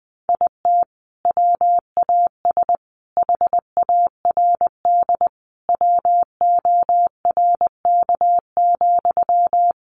Morse code, heard as a keyed tone, about 20 wpm, 700 Hz. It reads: IT WAS HARD WORK,
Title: Sentences from Top 600 Words 20wpm